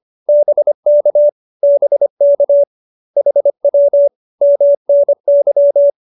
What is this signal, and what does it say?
Morse code, heard as a keyed tone, about 25 wpm, 570 Hz. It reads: BK BK HW MNY